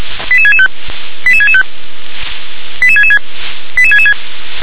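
Five-tone selective call (ZVEI), heard as quick runs of five tones, tones of about 70 ms. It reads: 88665 88665 88666 88666
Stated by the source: ZVEI five-tone selective call received on an SDR.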